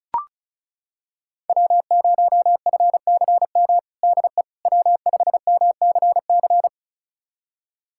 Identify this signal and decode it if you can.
Morse code, heard as a keyed tone, about 36 wpm, 700 Hz. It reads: W0FCM DE W5MCC
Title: USA Callsign DE Callsign - 5 Characters - 35wpm